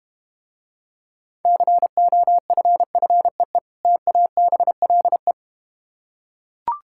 Morse code, heard as a keyed tone, about 32 wpm, 700 Hz. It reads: COFFEE TABLE